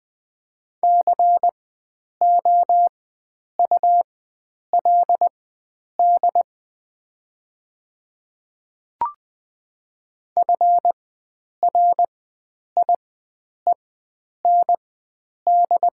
Morse code, heard as a keyed tone, about 20 wpm, spaced as for 9 wpm, 700 Hz. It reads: COULD FRIEND
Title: 4x Character Spacing - Top 500 Words - 20wpm